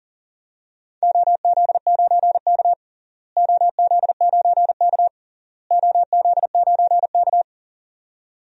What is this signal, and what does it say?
Morse code, heard as a keyed tone, about 40 wpm, 700 Hz. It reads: OZ9K OZ9K OZ9K
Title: DXCC-100 Call Signs - 4 Characters - Repeated 3x - 40wpm